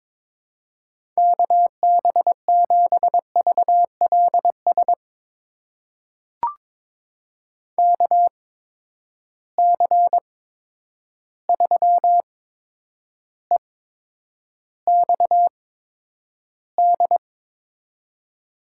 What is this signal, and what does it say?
Morse code, heard as a keyed tone, about 22 wpm, spaced as for 9 wpm, 700 Hz. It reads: KB7VLS KC3EXD
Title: USA Call Signs - 6 Characters - 8x Character Spacing - 22wpm